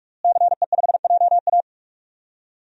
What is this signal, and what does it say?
Morse code, heard as a keyed tone, about 46 wpm, 700 Hz. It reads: CE5JA
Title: DXCC-100 Call Signs - 5 Characters - Repeated 3x - 45wpm